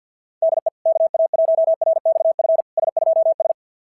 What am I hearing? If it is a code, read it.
Morse code, heard as a keyed tone, about 50 wpm, 660 Hz. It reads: DE KA1RXF S2S
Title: SOTA QSO Practice 50wpm